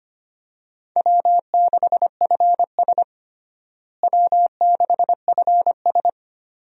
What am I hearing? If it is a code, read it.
Morse code, heard as a keyed tone, about 25 wpm, 700 Hz. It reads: W6FS W6FS
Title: USA Call Signs - 4 Characters - Repeated 2x - 25wpm